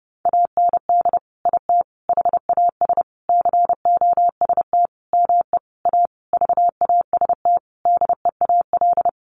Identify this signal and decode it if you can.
Morse code, heard as a keyed tone, about 30 wpm, 700 Hz. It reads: AND IT HAS COST ME A VAST DEAL
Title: Sentences from Top 1300 Words 30wpm